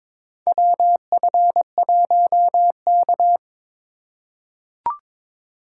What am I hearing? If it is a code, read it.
Morse code, heard as a keyed tone, about 22 wpm, 700 Hz. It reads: WF1K
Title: USA Call Signs - 4 Characters - 3x Character Spacing - 22wpm